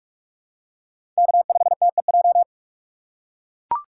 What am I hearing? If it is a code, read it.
Morse code, heard as a keyed tone, about 41 wpm, 700 Hz. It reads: K5TEJ